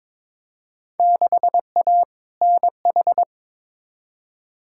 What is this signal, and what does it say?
Morse code, heard as a keyed tone, about 22 wpm, 700 Hz. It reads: 6A NH
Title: Field Day Exchanges - 22wpm